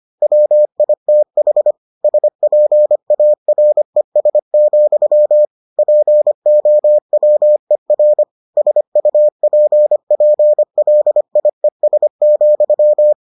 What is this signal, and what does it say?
Morse code, heard as a keyed tone, about 25 wpm, 600 Hz. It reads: WITH SPARES, POWER SUPPLIES,